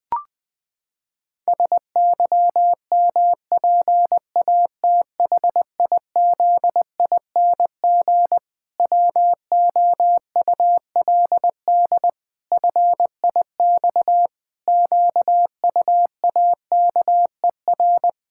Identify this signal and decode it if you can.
Morse code, heard as a keyed tone, about 20 wpm, 700 Hz. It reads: SYMPATHIZING WOULD FIX QUAKER
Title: Pangram Challenge v2 - 20wpm